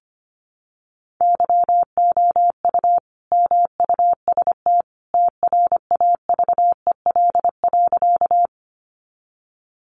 Morse code, heard as a keyed tone, about 25 wpm, 700 Hz. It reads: YOU MUST TRAVEL.
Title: Sentences from Top 400 Words 25wpm